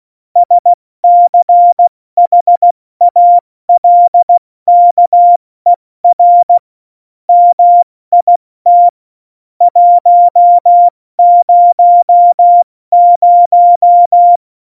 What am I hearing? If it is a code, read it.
Morse code, heard as a keyed tone, about 16 wpm, spaced as for 14 wpm, 700 Hz. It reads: SCHALKER MIT 100